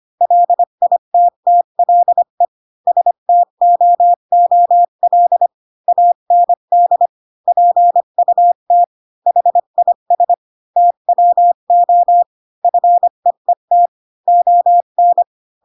Morse code, heard as a keyed tone, about 25 wpm, 700 Hz. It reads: LITTLE STOOL AND PUT HIS TWO FEET ON